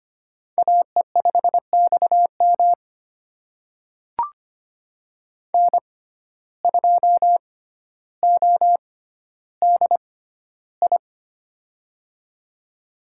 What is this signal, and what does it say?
Morse code, heard as a keyed tone, about 25 wpm, spaced as for 13 wpm, 700 Hz. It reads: AE5XM N2ODI
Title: USA Call Signs - 5 Characters - 6x Character Spacing - 25wpm